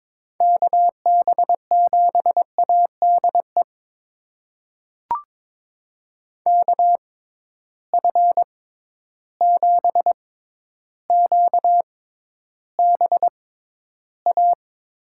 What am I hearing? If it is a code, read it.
Morse code, heard as a keyed tone, about 22 wpm, spaced as for 11 wpm, 700 Hz. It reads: KB7ADE KF7QBA
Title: USA Call Signs - 6 Characters - 6x Character Spacing - 22wpm